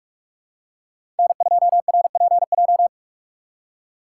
Morse code, heard as a keyed tone, about 44 wpm, 700 Hz. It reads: N2RPJ
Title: USA Call Signs - 5 Characters - 6x Character Spacing - 45wpm